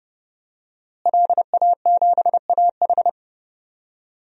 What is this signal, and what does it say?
Morse code, heard as a keyed tone, about 32 wpm, 700 Hz. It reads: LA7AH
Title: DXCC-100 Call Signs - 5 Characters - Repeated 2x - 30wpm